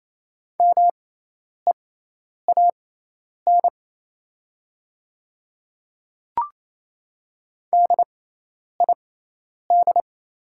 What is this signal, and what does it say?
Morse code, heard as a keyed tone, about 28 wpm, spaced as for 10 wpm, 700 Hz. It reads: MEAN DID
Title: 6x Character Spacing - Top 200 Words - 28wpm